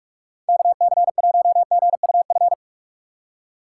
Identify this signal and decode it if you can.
Morse code, heard as a keyed tone, about 41 wpm, 700 Hz. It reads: KC1GUF
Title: USA Call Signs - 6 Characters - 4x Character Spacing - 45wpm